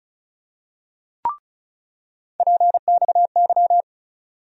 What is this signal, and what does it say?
Morse code, heard as a keyed tone, about 36 wpm, 700 Hz. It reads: PXY
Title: USA Call Sign Postfixes - 3 Letters 35wpm